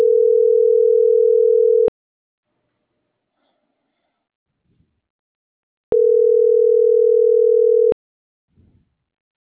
Ringback tone (North America), 440 + 480 Hz.